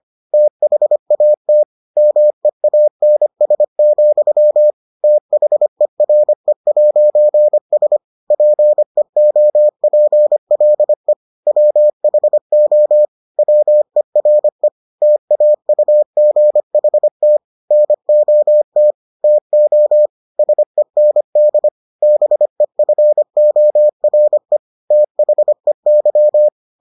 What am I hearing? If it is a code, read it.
Morse code, heard as a keyed tone, about 25 wpm, 600 Hz. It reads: THAT MEANS, THERE'S PEOPLE WHO WERE TAUGHT NOT TO SEND BEFORE THEY